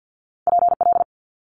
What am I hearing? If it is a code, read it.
Morse code, heard as a keyed tone, about 50 wpm, 700 Hz. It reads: PL
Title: Multi-letter Phonemes Quick Fire 50wpm